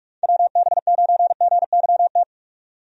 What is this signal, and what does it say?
Morse code, heard as a keyed tone, about 42 wpm, 700 Hz. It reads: WB9GYT
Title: USA Call Signs - 6 Characters - 45wpm